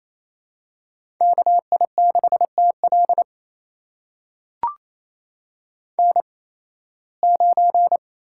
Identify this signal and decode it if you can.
Morse code, heard as a keyed tone, about 28 wpm, spaced as for 18 wpm, 700 Hz. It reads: KI6TL N9
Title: USA Call Signs - 5 Characters - 8x Character Spacing - 28wpm